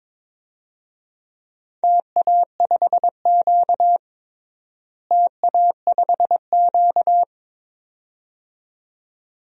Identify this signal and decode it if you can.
Morse code, heard as a keyed tone, about 22 wpm, 700 Hz. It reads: TA5Q TA5Q
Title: DXCC-100 Call Signs - 4 Characters - Repeated 2x - 22wpm